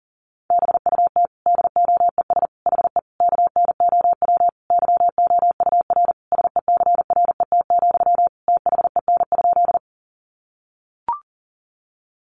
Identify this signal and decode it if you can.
Morse code, heard as a keyed tone, about 40 wpm, 700 Hz. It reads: BUT DOES HE KNOW YOUR SECRET, THEN?
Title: Sentences from Top 1200 Words 40wpm